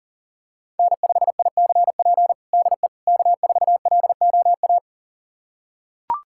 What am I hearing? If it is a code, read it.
Morse code, heard as a keyed tone, about 40 wpm, 700 Hz. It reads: N5ICP DE K4LOA
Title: USA Callsign DE Callsign - 5 Characters - 40wpm